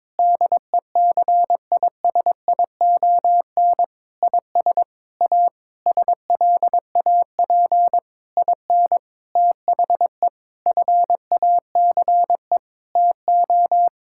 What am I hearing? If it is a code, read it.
Morse code, heard as a keyed tone, about 21 wpm, 700 Hz. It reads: DECISION IS A SLAP IN THE FACE TO
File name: practice / 22wpm / Sentences with 8-Word Phrases - 22wpm